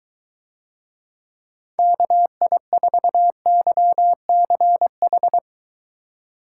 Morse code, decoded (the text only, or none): KI4YCH